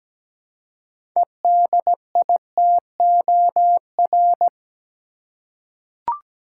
Morse code, heard as a keyed tone, about 17 wpm, 700 Hz. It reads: EDITOR
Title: Six Letter Words - 17wpm